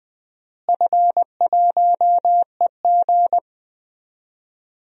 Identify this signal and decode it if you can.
Morse code, heard as a keyed tone, about 20 wpm, 700 Hz. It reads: F1EG
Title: DXCC-100 Call Signs - 4 Characters - Repeated 2x - 20wpm